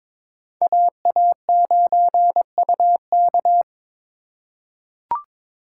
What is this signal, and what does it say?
Morse code, heard as a keyed tone, about 21 wpm, 700 Hz. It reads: AA9UK
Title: USA Call Signs - 5 Characters - 4x Character Spacing - 22wpm